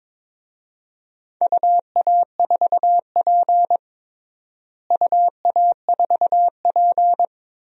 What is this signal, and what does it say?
Morse code, heard as a keyed tone, about 22 wpm, 700 Hz. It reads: UA4P UA4P
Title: DXCC-100 Call Signs - 4 Characters - Repeated 2x - 22wpm